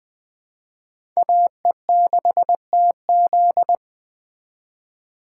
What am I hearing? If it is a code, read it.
Morse code, heard as a keyed tone, about 20 wpm, 700 Hz. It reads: AE6TZ
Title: Speed Racing - USA 2x2 Call Signs 17wpm